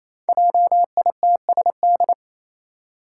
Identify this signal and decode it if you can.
Morse code, heard as a keyed tone, about 28 wpm, 700 Hz. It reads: JITSD
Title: Five Challenging Letter-Number Combinations 28wpm